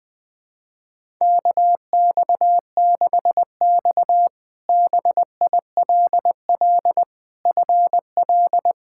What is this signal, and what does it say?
Morse code, heard as a keyed tone, about 20 wpm, 700 Hz. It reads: KX6X BILL FL
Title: K1USN SST QSO Practice - 20wpm